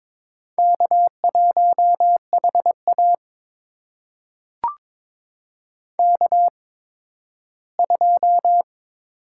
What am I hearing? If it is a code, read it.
Morse code, heard as a keyed tone, about 22 wpm, spaced as for 13 wpm, 700 Hz. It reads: K1HA K2
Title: USA Call Signs - 4 Characters - 8x Character Spacing - 22wpm